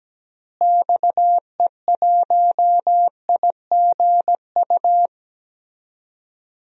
Morse code, heard as a keyed tone, about 17 wpm, 700 Hz. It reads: XE1IGU
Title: DXCC-100 Call Signs - 6 Characters - Repeated 3x - 17wpm